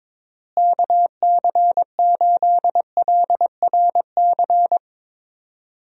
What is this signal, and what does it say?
Morse code, heard as a keyed tone, about 22 wpm, 700 Hz. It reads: KC8LRC
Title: USA Call Signs - 6 Characters - Repeated 3x - 22wpm